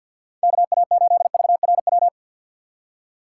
Morse code, heard as a keyed tone, about 52 wpm, 700 Hz. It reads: KA8VRW
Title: USA Call Signs - 2x Character Spacing - 50wpm